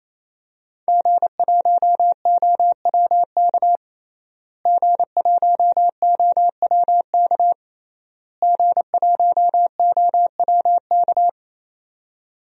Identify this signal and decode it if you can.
Morse code, heard as a keyed tone, about 27 wpm, 700 Hz. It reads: G1OWK G1OWK G1OWK
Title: DXCC-100 Call Signs - 5 Characters - Repeated 3x - 28wpm